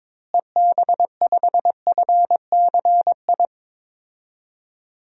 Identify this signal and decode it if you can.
Morse code, heard as a keyed tone, about 22 wpm, 700 Hz. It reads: EB5FCI
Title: DXCC-100 Call Signs - 6 Characters - Repeated 3x - 22wpm